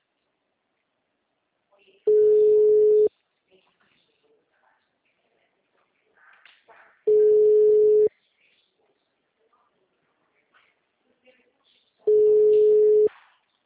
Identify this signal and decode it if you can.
Ringback tone (Europe, 425 Hz).